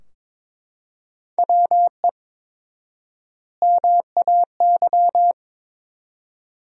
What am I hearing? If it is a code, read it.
Morse code, heard as a keyed tone, about 23 wpm, 700 Hz. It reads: WE MAY
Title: 4x Word-Spacing - Sentences from Top 300 Words - 22wpm